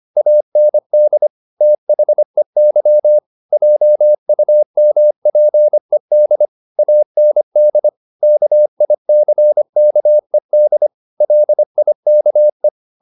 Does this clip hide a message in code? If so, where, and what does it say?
Yes: Morse code from 0.2 to 12.7 s, heard as a keyed tone, about 25 wpm, 600 Hz. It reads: AND THEY JUMPED AND KICKED LIKE